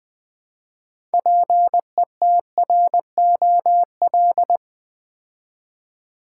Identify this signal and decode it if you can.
Morse code, heard as a keyed tone, about 20 wpm, 700 Hz. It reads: PETROL
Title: Build A Word - 2 Letters At A Time - 6-Letter Words - 20wpm